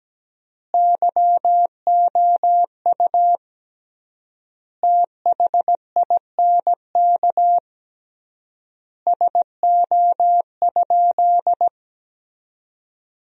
Morse code, heard as a keyed tone, about 17 wpm, 700 Hz. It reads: YOU THINK SO?
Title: 3x Word-Spacing - Sentences from Top 300 Words - 17wpm